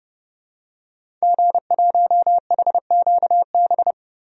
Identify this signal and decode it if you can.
Morse code, heard as a keyed tone, about 30 wpm, 700 Hz. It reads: G1HQB
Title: DXCC-100 Call Signs - 5 Characters - Repeated 3x - 30wpm